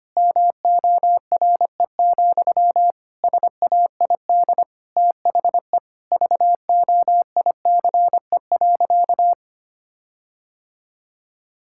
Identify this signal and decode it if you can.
Morse code, heard as a keyed tone, about 25 wpm, 700 Hz. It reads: MORE, SAID THE VOICE.
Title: Sentences from Top 1600 Words 25wpm